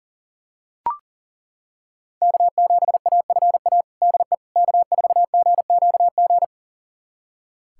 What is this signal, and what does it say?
Morse code, heard as a keyed tone, about 40 wpm, 700 Hz. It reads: K7AFA DE K4GQG
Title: USA Callsign DE Callsign - 5 Characters - 40wpm